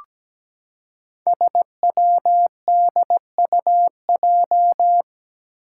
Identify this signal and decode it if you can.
Morse code, heard as a keyed tone, about 17 wpm, 700 Hz. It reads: SWDUJ
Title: Five Challenging Letter-Number Combinations 17wpm